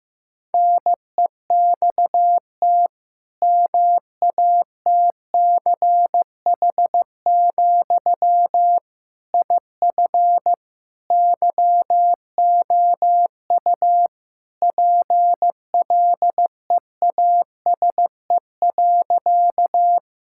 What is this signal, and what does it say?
Morse code, heard as a keyed tone, about 15 wpm, 700 Hz. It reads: NEXT MATCH, IF YOU PLEASE.
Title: Sentences from Top 1600 Words 15wpm